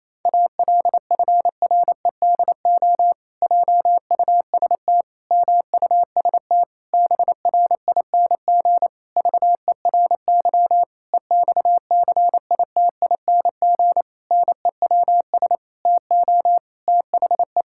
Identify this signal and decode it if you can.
Morse code, heard as a keyed tone, about 28 wpm, 700 Hz. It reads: ALFREDO JUST MUST BRING VERY EXCITING NEWS TO THE
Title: Pangram Challenge v2 - 28wpm